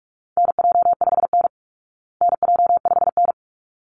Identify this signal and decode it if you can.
Morse code, heard as a keyed tone, about 44 wpm, 700 Hz. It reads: NJ5N NJ5N
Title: Call Signs Repeated 3x - 45wpm